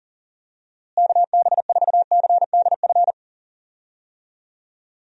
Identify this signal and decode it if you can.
Morse code, heard as a keyed tone, about 40 wpm, 700 Hz. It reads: KB4CDF
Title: USA Call Signs - 6 Characters - Repeated 2x - 40wpm